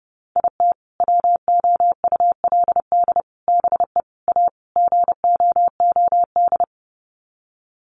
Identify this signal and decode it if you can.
Morse code, heard as a keyed tone, about 30 wpm, 700 Hz. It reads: IT WOULD BE A GOOD
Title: Sets of 5 Words Encore2 30wpm